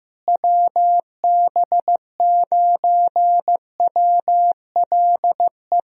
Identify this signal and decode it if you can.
Morse code, heard as a keyed tone, about 16 wpm, 700 Hz. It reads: WB9WLE
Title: USA Call Signs - 6 Characters - Repeated 3x - 15wpm